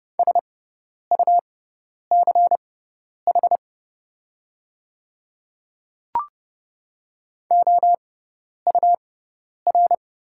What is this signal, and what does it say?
Morse code, heard as a keyed tone, about 30 wpm, spaced as for 10 wpm, 700 Hz. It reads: SUCH OUR
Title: 6x Character Spacing - Top 300 Words - 30wpm